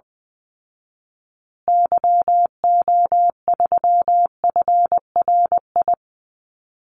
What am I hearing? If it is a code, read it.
Morse code, heard as a keyed tone, about 20 wpm, 700 Hz. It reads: YO3FRI
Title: Call Signs Repeated 3x - 20wpm